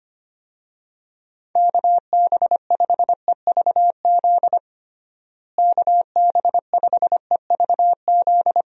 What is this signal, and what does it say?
Morse code, heard as a keyed tone, about 25 wpm, 700 Hz. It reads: KB5EVZ KB5EVZ
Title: USA Call Signs - 6 Characters - Repeated 2x - 25wpm